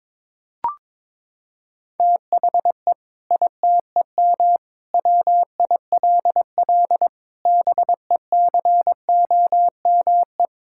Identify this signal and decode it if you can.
Morse code, heard as a keyed tone, about 22 wpm, 700 Hz. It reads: THE ITEM WILL BECOME
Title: Sentences from Top 1000 Words - Review Part 6 - 22wpm